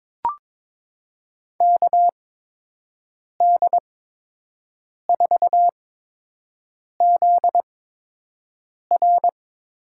Morse code, heard as a keyed tone, about 22 wpm, spaced as for 6 wpm, 700 Hz. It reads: KD4ZR
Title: USA Call Signs - 6 Characters - 8x Character Spacing - 22wpm